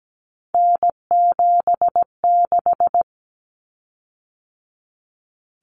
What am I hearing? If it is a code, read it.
Morse code, heard as a keyed tone, about 17 wpm, 700 Hz. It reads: N76
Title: Common Cut Numbers - 3-Digit 17wpm